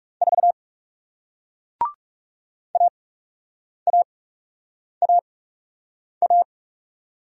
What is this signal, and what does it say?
Morse code, heard as a keyed tone, about 41 wpm, spaced as for 8 wpm, 700 Hz. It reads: 4 AAAA